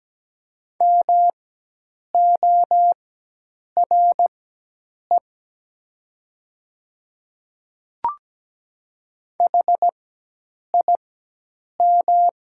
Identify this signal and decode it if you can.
Morse code, heard as a keyed tone, about 17 wpm, spaced as for 8 wpm, 700 Hz. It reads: MORE HIM
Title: 4x Character Spacing - Top 100 Words - 17wpm